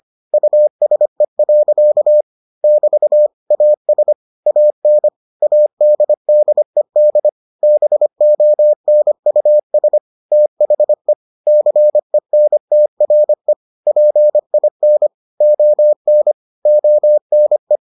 Morse code, heard as a keyed tone, about 25 wpm, 600 Hz. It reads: USE. <BT> AS AN ADDED BONUS THE CENTRE PIN ON ONE